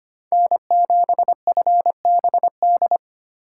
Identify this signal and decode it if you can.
Morse code, heard as a keyed tone, about 25 wpm, 700 Hz. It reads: N7FBD
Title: USA Call Signs - 5 Characters - 4x Character Spacing - 25wpm